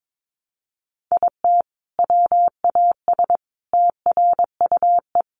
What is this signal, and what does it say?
Morse code, heard as a keyed tone, about 22 wpm, 700 Hz. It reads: IT WAS TRUE
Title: Sentences from Top 400 Words 22wpm